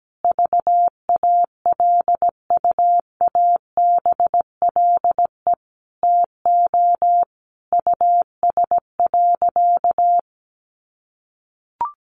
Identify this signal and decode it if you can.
Morse code, heard as a keyed tone, about 17 wpm, 700 Hz. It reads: VALUABLE TO US.
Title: Sentences from Top 1100 Words 17wpm